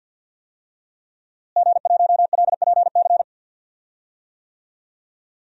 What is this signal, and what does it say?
Morse code, heard as a keyed tone, about 52 wpm, 700 Hz. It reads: G1LPC
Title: DXCC-100 Call Signs - 5 Characters - Repeated 2x - 50wpm